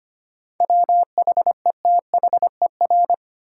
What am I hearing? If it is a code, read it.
Morse code, heard as a keyed tone, about 25 wpm, 700 Hz. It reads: WHETHER
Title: Sentences from Top 1000 Words - Review Part 3 - New Word Drill - 25wpm